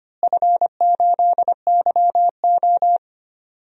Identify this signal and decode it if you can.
Morse code, heard as a keyed tone, about 25 wpm, 700 Hz. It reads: F8YO
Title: DXCC-100 Call Signs - 4 Characters - Repeated 3x - 25wpm